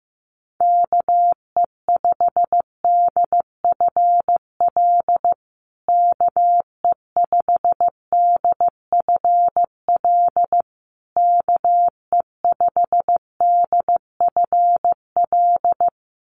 Morse code, heard as a keyed tone, about 15 wpm, 700 Hz. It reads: KE5DFL KE5DFL KE5DFL
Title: USA Call Signs Repeated 3x 15wpm